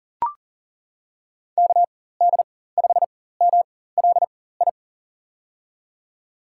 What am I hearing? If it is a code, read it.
Morse code, heard as a keyed tone, about 40 wpm, spaced as for 19 wpm, 700 Hz. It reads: KD5MLI
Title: USA Call Signs - 4x Character Spacing - 40wpm